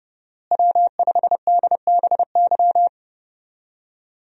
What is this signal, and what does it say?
Morse code, heard as a keyed tone, about 30 wpm, 700 Hz. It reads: W5DBY
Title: USA Call Signs - 5 Characters - Repeated 2x - 30wpm